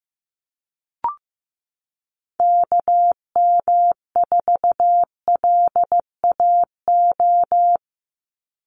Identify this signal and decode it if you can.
Morse code, heard as a keyed tone, about 15 wpm, 700 Hz. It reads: KM4LAO
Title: Call Signs Repeated 3x - 15wpm